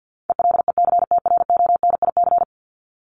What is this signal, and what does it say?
Morse code, heard as a keyed tone, about 50 wpm, 700 Hz. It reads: ELECTRONIC
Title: Top 1001-1250 Words 50wpm